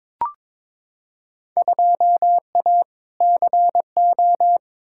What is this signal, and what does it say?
Morse code, heard as a keyed tone, about 22 wpm, 700 Hz. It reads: 2A CO
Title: Field Day Exchanges - 22wpm